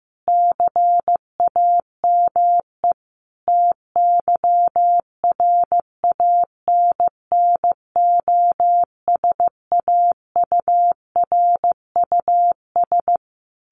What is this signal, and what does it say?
Morse code, heard as a keyed tone, about 15 wpm, 700 Hz. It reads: CAME TYRANNOSAURUS
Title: A Sound Of Thunder 15wpm